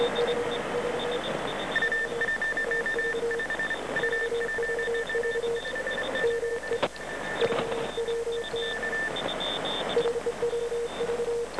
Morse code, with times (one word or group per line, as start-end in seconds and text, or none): none